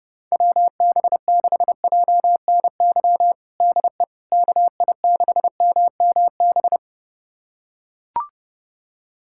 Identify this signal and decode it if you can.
Morse code, heard as a keyed tone, about 30 wpm, 700 Hz. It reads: WB6JNY DE KI6MMB
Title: USA Callsign DE Callsign 30wpm